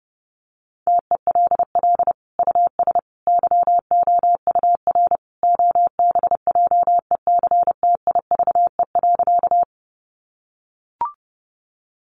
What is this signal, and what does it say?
Morse code, heard as a keyed tone, about 28 wpm, 700 Hz. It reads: TELL US YOUR OBJECTIVE.
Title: Sentences from Top 1200 Words 30wpm